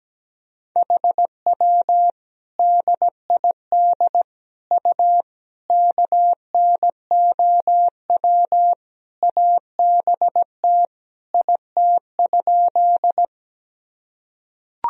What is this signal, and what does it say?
Morse code, heard as a keyed tone, about 17 wpm, 700 Hz. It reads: HW DID U KNOW ABT IT?
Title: Sentences from Top 100 Words and Abbreviations 17wpm